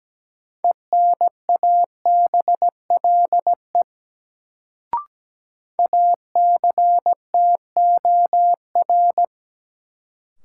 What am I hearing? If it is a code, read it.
Morse code, heard as a keyed tone, about 17 wpm, 700 Hz. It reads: ENABLE ACTOR
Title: Sentences from Top 1500 Words - New Word Drill Alt - 17wpm